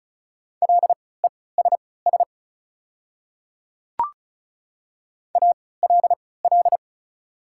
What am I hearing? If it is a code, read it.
Morse code, heard as a keyed tone, about 34 wpm, spaced as for 20 wpm, 700 Hz. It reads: LESS ALL
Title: 3x Character Spacing - Top 500 Words - 35wpm